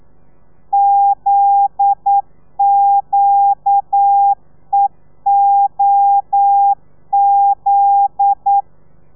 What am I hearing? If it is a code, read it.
Morse code, heard as a keyed tone, about 9 wpm, 800 Hz. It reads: ZQEOZ
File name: telegrafia / i2tfj / Lez9 40